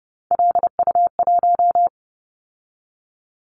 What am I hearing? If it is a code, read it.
Morse code, heard as a keyed tone, about 30 wpm, 700 Hz. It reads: LU1
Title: Callsign Prefixes for Top 100 DXCC - 30wpm